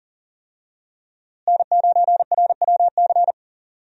Morse code, heard as a keyed tone, about 40 wpm, 700 Hz. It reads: N9RWC
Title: USA Call Signs - 5 Characters - Repeated 3x - 40wpm